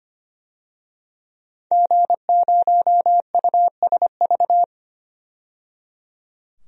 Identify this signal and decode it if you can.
Morse code, heard as a keyed tone, about 25 wpm, 700 Hz. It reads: G0USV
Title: DXCC-100 Call Signs - 5 Characters - 25wpm